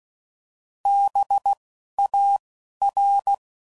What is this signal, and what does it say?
Morse code, heard as a keyed tone, about 16 wpm, 800 Hz. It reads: B A R